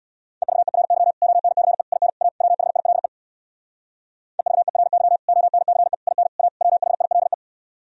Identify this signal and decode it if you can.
Morse code, heard as a keyed tone, about 64 wpm, 700 Hz. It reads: EVERY ONCE IN A WHILE EVERY ONCE IN A WHILE